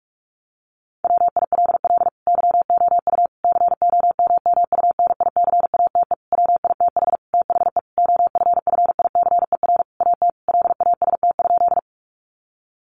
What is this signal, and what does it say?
Morse code, heard as a keyed tone, about 44 wpm, 700 Hz. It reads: WILL YOU COMMUNICATE WITH THE OFFICER AT LAST?
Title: Sentences from Top 1300 Words 45wpm